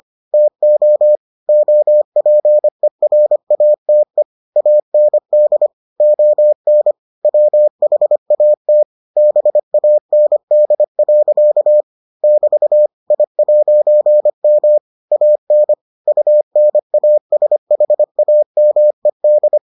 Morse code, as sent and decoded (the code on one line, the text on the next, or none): - --- / --- .--. .-. .- - . / .- -. -.. / --- -. / .-- .... .- - / -... .- -. -.. .-.-.- / -...- / .. .----. -- / .- -. / ..- -. .- ... .... .- -- . -..
TO OPERATE AND ON WHAT BAND. <BT> I'M AN UNASHAMED